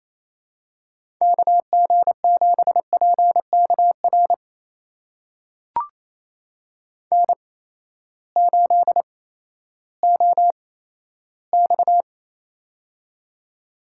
Morse code, heard as a keyed tone, about 28 wpm, spaced as for 14 wpm, 700 Hz. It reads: KG7PKR N8OX
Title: USA Call Signs - 8x Character Spacing - 28wpm